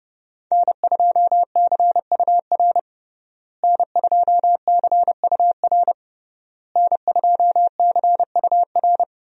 Morse code, heard as a keyed tone, about 32 wpm, 700 Hz. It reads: N2CUR N2CUR N2CUR
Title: USA Call Signs - 5 Characters - Repeated 3x - 30wpm